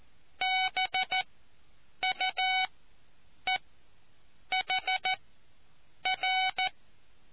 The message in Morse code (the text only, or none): BUEHR